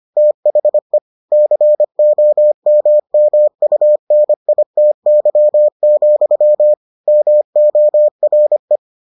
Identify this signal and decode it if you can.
Morse code, heard as a keyed tone, about 25 wpm, 600 Hz. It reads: THE COMMUNITY, MORE